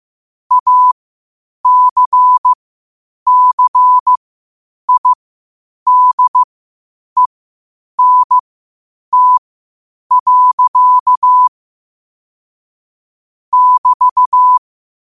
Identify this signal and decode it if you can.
Morse code, heard as a keyed tone, about 15 wpm, spaced as for 8 wpm, 1000 Hz. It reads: ACCIDENT. <BT>
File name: qotdfiles / 8wpm